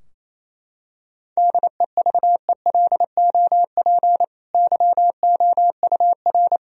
Morse code, heard as a keyed tone, about 28 wpm, 700 Hz. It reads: DEVELOP YOUR
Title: Sentences from Top 1000 Words - Review Part 2 - 28wpm